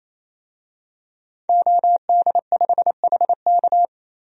Morse code, heard as a keyed tone, about 27 wpm, 700 Hz. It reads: OD5HK
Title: DXCC-100 Call Signs - 5 Characters - Repeated 3x - 28wpm